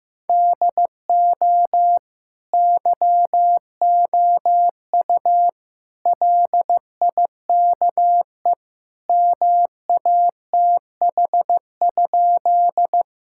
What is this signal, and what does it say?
Morse code, heard as a keyed tone, about 15 wpm, 700 Hz. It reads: DO YOU LIKE MATH?
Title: Sentences from Top 1000 Words - Review Part 6 - 15wpm